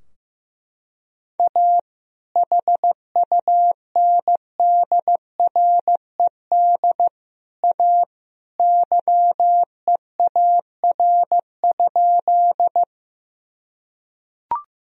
Morse code, heard as a keyed tone, about 15 wpm, 700 Hz. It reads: A HUNDRED A YEAR?
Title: Sentences from Top 1000 Words - Review Part 2 - 15wpm